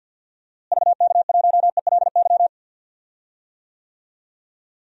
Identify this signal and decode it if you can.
Morse code, heard as a keyed tone, about 50 wpm, 700 Hz. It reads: VK1ELY
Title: DXCC-100 Call Signs - 6 Characters - Repeated 3x - 50wpm